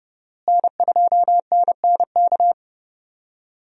Morse code, heard as a keyed tone, about 30 wpm, 700 Hz. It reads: N2NNK
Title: USA Call Signs - 5 Characters - Repeated 2x - 30wpm